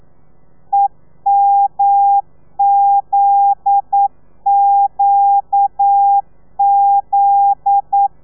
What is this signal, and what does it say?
Morse code, heard as a keyed tone, about 9 wpm, 800 Hz. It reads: EMZQZ
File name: telegrafia / i2tfj / Lez9 40